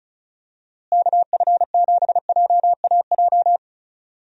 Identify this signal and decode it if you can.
Morse code, heard as a keyed tone, about 34 wpm, 700 Hz. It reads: KF7JAJ
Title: USA Call Signs - 6 Characters - Repeated 2x - 35wpm